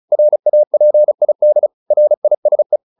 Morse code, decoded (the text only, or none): RAPID RISE